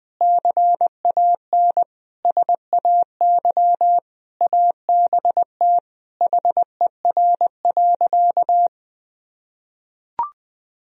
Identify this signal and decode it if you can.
Morse code, heard as a keyed tone, about 21 wpm, 700 Hz. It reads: CAN SAY ABT HER.
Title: Sentences from Top 200 Words and Abbreviations 20wpm